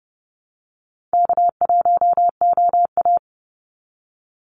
Morse code, heard as a keyed tone, about 30 wpm, 700 Hz. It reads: K1OA
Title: Call Signs Repeated 2x - 30wpm